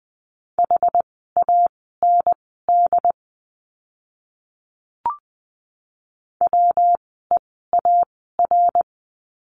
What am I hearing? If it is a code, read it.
Morse code, heard as a keyed tone, about 20 wpm, spaced as for 14 wpm, 700 Hz. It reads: HAND WEAR
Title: 115 Lesson 19f - Letter w - Words 20wpm